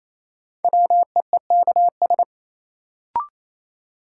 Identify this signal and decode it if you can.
Morse code, heard as a keyed tone, about 27 wpm, 700 Hz. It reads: WEEKS
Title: Sets of 5 Words - New Word Drill - 28wpm